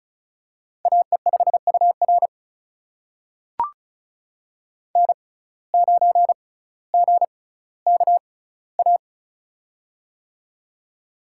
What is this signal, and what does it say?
Morse code, heard as a keyed tone, about 36 wpm, spaced as for 18 wpm, 700 Hz. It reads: AE5UR N9GKA
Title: USA Call Signs - 5 Characters - 6x Character Spacing - 35wpm